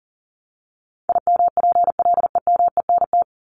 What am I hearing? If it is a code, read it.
Morse code, heard as a keyed tone, about 40 wpm, 700 Hz. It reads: IMPLEMENT
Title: Top 1501-1750 Words 40wpm